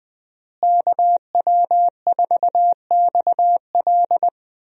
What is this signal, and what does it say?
Morse code, heard as a keyed tone, about 20 wpm, 700 Hz. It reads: KW4XL
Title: USA Call Signs - 5 Characters - Repeated 3x - 20wpm